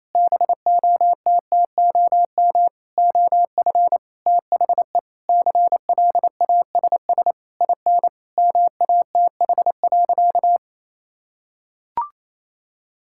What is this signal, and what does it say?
Morse code, heard as a keyed tone, about 28 wpm, 700 Hz. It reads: BOTTOM OF THE CLASS IN MATH.
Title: Sentences from Top 1000 Words - Review Part 6 - 28wpm